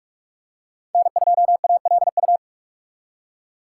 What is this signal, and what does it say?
Morse code, heard as a keyed tone, about 46 wpm, 700 Hz. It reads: N2ALU
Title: USA Call Signs - 5 Characters - 3x Character Spacing - 45wpm